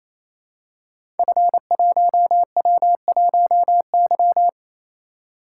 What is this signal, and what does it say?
Morse code, heard as a keyed tone, about 27 wpm, 700 Hz. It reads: F1W1Y
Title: Code Groups - 5 Letter:Numbers - 28wpm